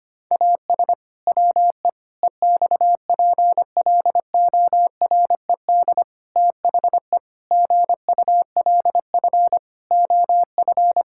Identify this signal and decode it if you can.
Morse code, heard as a keyed tone, about 25 wpm, 700 Hz. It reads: AS WE EXPLORED THE GULF OF
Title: Pangram Challenge v2 - 25wpm